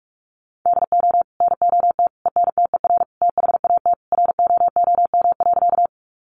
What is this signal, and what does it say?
Morse code, heard as a keyed tone, about 44 wpm, 700 Hz. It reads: DO NOT ENTER THAT ROOM.
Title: Sentences from Top 1300 Words 45wpm